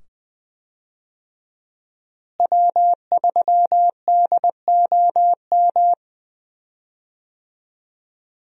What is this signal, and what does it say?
Morse code, heard as a keyed tone, about 20 wpm, 700 Hz. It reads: W3DOM
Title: USA Call Signs - 5 Characters - Repeated 2x - 20wpm